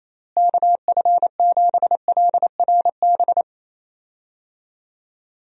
Morse code, heard as a keyed tone, about 28 wpm, 700 Hz. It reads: KF7LRB
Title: USA Call Signs - 6 Characters - Repeated 2x - 28wpm